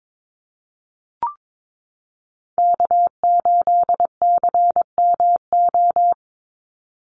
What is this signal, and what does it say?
Morse code, heard as a keyed tone, about 22 wpm, 700 Hz. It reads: K8CMO
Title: Call Signs - 22wpm